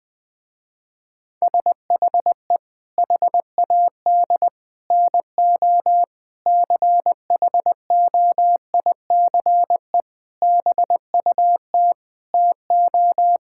Morse code, heard as a keyed tone, about 20 wpm, 700 Hz. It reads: SHE HAD NO CHOICE BUT TO
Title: Sentences with 5-Word Phrases - Spoken Beginning - 20wpm